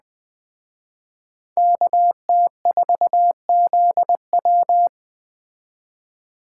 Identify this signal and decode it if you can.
Morse code, heard as a keyed tone, about 21 wpm, 700 Hz. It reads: KT4ZW